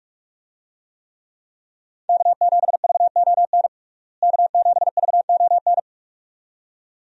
Morse code, heard as a keyed tone, about 42 wpm, 700 Hz. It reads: K7VON K7VON
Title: USA Call Signs - 5 Characters - Repeated 2x - 45wpm